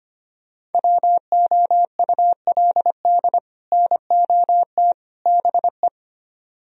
Morse code, heard as a keyed tone, about 33 wpm, 700 Hz. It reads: WOULD NOT BE